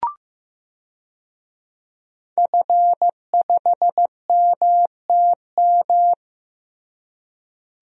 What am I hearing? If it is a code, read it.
Morse code, heard as a keyed tone, about 15 wpm, 700 Hz. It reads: F5MTM
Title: DXCC-100 Call Signs - 5 Characters - Repeated 3x - 15wpm